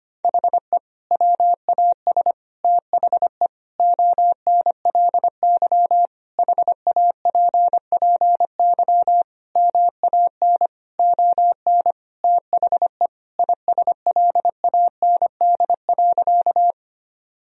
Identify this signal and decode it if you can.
Morse code, heard as a keyed tone, about 25 wpm, 700 Hz. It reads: HE WAS THE ONLY HAPPY MAN ON THE ISLAND.
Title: Sentences from Top 1000 Words - Review Part 3 - 25wpm